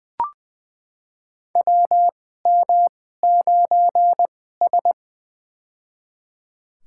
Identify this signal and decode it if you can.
Morse code, heard as a keyed tone, about 20 wpm, spaced as for 15 wpm, 700 Hz. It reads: WM9S
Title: USA Call Signs - 4 Characters - 2x Character Spacing - 20wpm